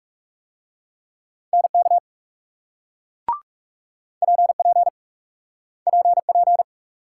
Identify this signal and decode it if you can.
Morse code, heard as a keyed tone, about 38 wpm, 700 Hz. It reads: NK PP PP